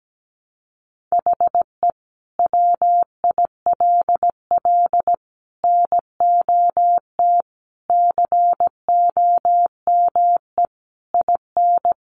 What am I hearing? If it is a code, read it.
Morse code, heard as a keyed tone, about 17 wpm, 700 Hz. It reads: HE WILL NOT COME IN
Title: Sentences from Top 600 Words 17wpm